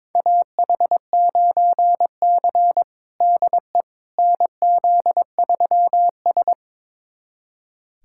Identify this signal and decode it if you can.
Morse code, heard as a keyed tone, about 22 wpm, 700 Hz. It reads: AH9C DE NZ3S
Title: USA Callsign DE Callsign - 4 Characters - 22wpm